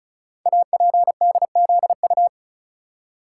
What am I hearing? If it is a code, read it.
Morse code, heard as a keyed tone, about 36 wpm, 690 Hz. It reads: APDZU
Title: Code Groups - 5 Letters - 35wpm